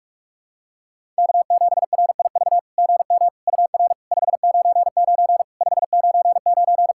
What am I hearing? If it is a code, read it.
Morse code, heard as a keyed tone, about 48 wpm, 700 Hz. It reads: K7RIV GM UR 599 599
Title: SOTA QSO Practice 45wpm